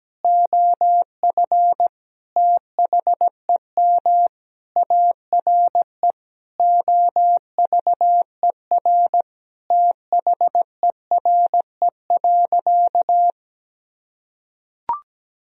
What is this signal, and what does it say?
Morse code, heard as a keyed tone, about 17 wpm, 700 Hz. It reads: OF THEM ARE OVER THERE.
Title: Sentences from Top 1000 Words - Review Part 2 - 17wpm